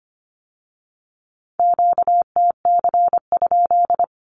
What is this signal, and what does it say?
Morse code, heard as a keyed tone, about 25 wpm, 700 Hz. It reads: QTC?
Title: Q-Codes with Spoken Meaning - 25wpm